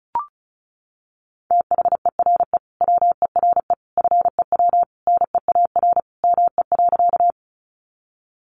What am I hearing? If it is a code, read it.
Morse code, heard as a keyed tone, about 34 wpm, 700 Hz. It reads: THERE WERE FEW NEAR ME.
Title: Sentences from Top 300 Words 35wpm